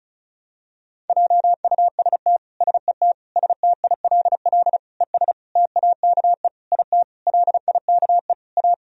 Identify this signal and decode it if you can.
Morse code, heard as a keyed tone, about 36 wpm, 700 Hz. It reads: JUST SET STILL ES TAKE IT LIKE A
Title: Sentences from Top 300 Words and Abbreviations 35wpm